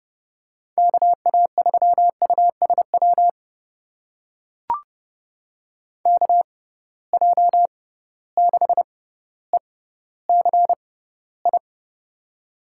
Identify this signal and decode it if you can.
Morse code, heard as a keyed tone, about 30 wpm, spaced as for 15 wpm, 700 Hz. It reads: KA3USW KJ6ECI
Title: USA Call Signs - 6 Characters - 6x Character Spacing - 30wpm